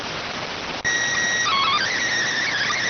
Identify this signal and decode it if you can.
SSTV (slow-scan television), PD 290.